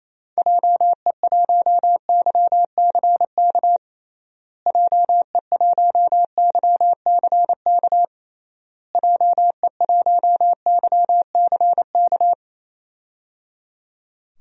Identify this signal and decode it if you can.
Morse code, heard as a keyed tone, about 28 wpm, 700 Hz. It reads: JE1YCK JE1YCK JE1YCK
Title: DXCC-100 Call Signs - 6 Characters - Repeated 3x - 28wpm